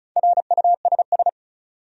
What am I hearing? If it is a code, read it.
Morse code, heard as a keyed tone, about 34 wpm, 700 Hz. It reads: RUSS
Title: USA Common Names - Max 4 Letters - 35wpm